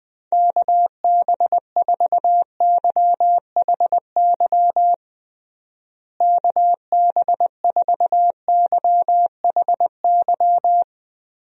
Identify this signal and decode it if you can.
Morse code, heard as a keyed tone, about 21 wpm, 700 Hz. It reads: KB4YHY KB4YHY
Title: USA Call Signs - 6 Characters - Repeated 3x - 20wpm